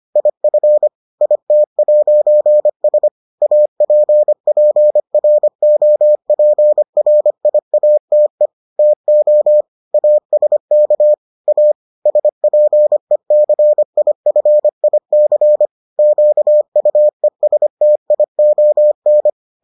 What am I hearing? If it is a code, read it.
Morse code, heard as a keyed tone, about 25 wpm, 600 Hz. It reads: IF IT'S APPROPRIATE TO ASK A SPECIFIC QUESTION